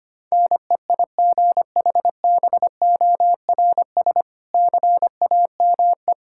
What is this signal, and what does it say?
Morse code, heard as a keyed tone, about 25 wpm, 700 Hz. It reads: NEIGHBORS CAME
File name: practice / 25wpm / Difficult Sentences with Hints - 25wpm